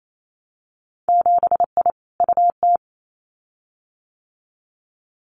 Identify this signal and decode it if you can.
Morse code, heard as a keyed tone, about 28 wpm, 700 Hz. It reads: 7I UT
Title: Winter Field Day Exchanges Encore 28wpm